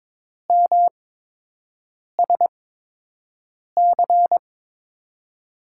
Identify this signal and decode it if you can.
Morse code, heard as a keyed tone, about 22 wpm, spaced as for 6 wpm, 700 Hz. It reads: MSC
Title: USA Call Signs - 8x Character Spacing - 22wpm